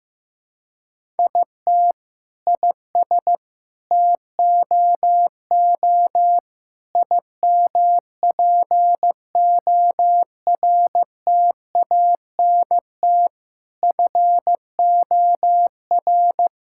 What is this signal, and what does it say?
Morse code, heard as a keyed tone, about 16 wpm, 700 Hz. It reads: IT IS TOO IMPORTANT FOR
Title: Sentences from Top 1000 Words - Review Part 6 - 15wpm